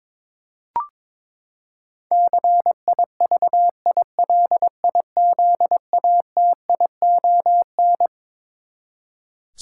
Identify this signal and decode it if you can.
Morse code, heard as a keyed tone, about 22 wpm, 700 Hz. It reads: CIVILIZATION
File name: Twelve Letter Words - 22wpm